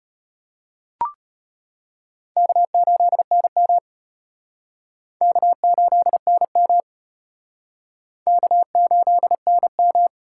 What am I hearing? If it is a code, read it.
Morse code, heard as a keyed tone, about 30 wpm, 700 Hz. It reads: K8NM K8NM K8NM
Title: Speed Racing - USA 1x2 Call Signs 25wpm